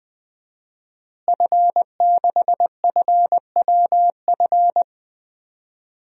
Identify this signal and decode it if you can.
Morse code, heard as a keyed tone, about 20 wpm, 700 Hz. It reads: F6FWF
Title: DXCC-100 Call Signs - 5 Characters - Repeated 2x - 20wpm